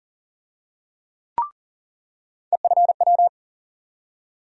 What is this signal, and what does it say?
Morse code, heard as a keyed tone, about 40 wpm, 700 Hz. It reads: EFW
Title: USA Call Sign Postfixes - 3 Letters 40wpm